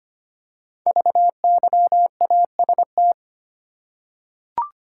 Morse code, heard as a keyed tone, about 25 wpm, 700 Hz. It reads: VYAST
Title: Code Groups - 5 Letters - 25wpm